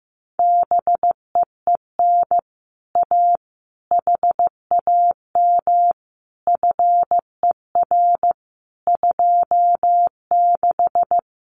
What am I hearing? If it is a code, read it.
Morse code, heard as a keyed tone, about 15 wpm, 700 Hz. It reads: BEEN A HAM FER 26
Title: Common QSO Phrases Encore 15wpm